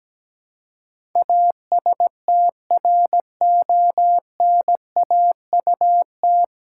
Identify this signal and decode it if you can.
Morse code, heard as a keyed tone, about 17 wpm, 700 Hz. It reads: ASTRONAUT
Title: Pangram Challenge v2 - 17wpm